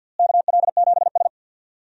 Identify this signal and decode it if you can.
Morse code, heard as a keyed tone, about 45 wpm, 700 Hz. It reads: KL7S
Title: DXCC-100 Call Signs - 4 Characters - Repeated 2x - 50wpm